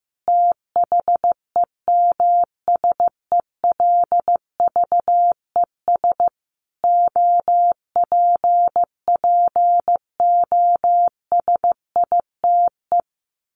Morse code, heard as a keyed tone, about 15 wpm, 700 Hz. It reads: THEMSELVES OPPOSITE
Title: A Sound Of Thunder 15wpm